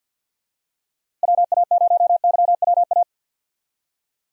Morse code, heard as a keyed tone, about 52 wpm, 700 Hz. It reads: WA0YWA
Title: USA Call Signs - 6 Characters - Repeated 2x - 50wpm